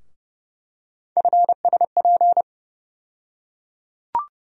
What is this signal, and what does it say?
Morse code, heard as a keyed tone, about 32 wpm, 700 Hz. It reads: FSP
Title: USA Call Sign Postfixes - 3 Letters 30wpm